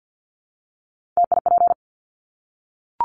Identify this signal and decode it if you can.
Morse code, heard as a keyed tone, about 50 wpm, 700 Hz. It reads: TIP
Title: Top 1501-1750 Words 50wpm